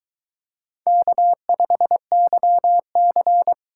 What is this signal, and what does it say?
Morse code, heard as a keyed tone, about 22 wpm, 700 Hz. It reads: K5YC